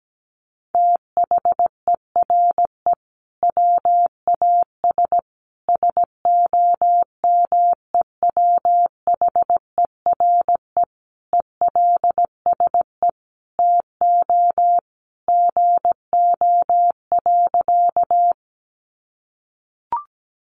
Morse code, heard as a keyed tone, about 17 wpm, 700 Hz. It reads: THERE WAS SOMEWHERE ELSE TO GO.
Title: Sentences from Top 1400 Words 17wpm